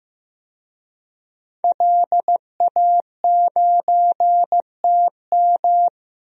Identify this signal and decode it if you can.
Morse code, heard as a keyed tone, about 16 wpm, 700 Hz. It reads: LA9TM
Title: DXCC-100 Call Signs - 5 Characters - Repeated 2x - 15wpm